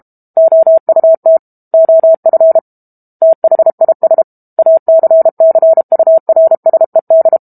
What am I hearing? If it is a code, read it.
Morse code, heard as a keyed tone, about 32 wpm, 650 Hz. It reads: OUT OF THIS ACCURSED
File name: Chapter 6 - 32 WPM